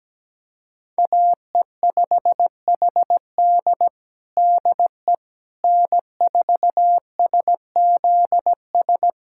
Morse code, heard as a keyed tone, about 17 wpm, 700 Hz. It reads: AE5HD DE N4SZS
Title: USA Callsign DE Callsign - 5 Characters - 17wpm